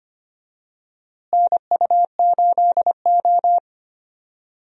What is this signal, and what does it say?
Morse code, heard as a keyed tone, about 25 wpm, 700 Hz. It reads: NU8O
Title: USA Call Signs - 4 Characters - Repeated 3x - 25wpm